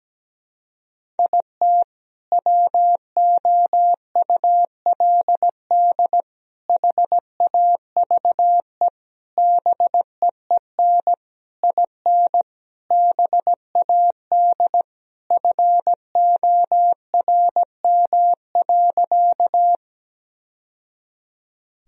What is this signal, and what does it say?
Morse code, heard as a keyed tone, about 17 wpm, 700 Hz. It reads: IT WOULD HAVE BEEN IN BAD FORM.
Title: Sentences from Top 1000 Words - Review Part 1 - 17wpm